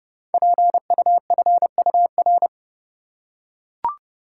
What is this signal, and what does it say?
Morse code, heard as a keyed tone, about 30 wpm, 700 Hz. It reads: PUFUR
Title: Code Groups - 5 Letters - 30wpm